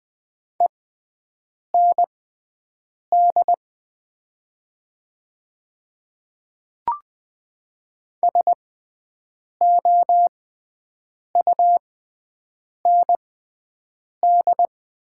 Morse code, heard as a keyed tone, about 20 wpm, spaced as for 7 wpm, 700 Hz. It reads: END SOUND
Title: 6x Character Spacing - Top 200 Words - 20wpm